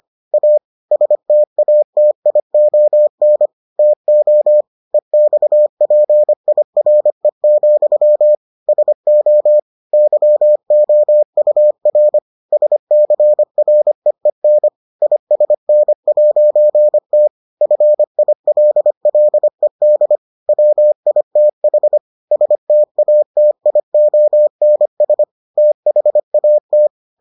Morse code, heard as a keyed tone, about 25 wpm, 600 Hz. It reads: A STATION TO EXPIRE, SO YOUR SCREEN ISN'T FILLED WITH STATIONS THAT